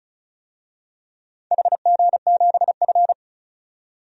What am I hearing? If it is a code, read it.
Morse code, heard as a keyed tone, about 36 wpm, 700 Hz. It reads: HG7F
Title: DXCC-100 Call Signs - 4 Characters - 35wpm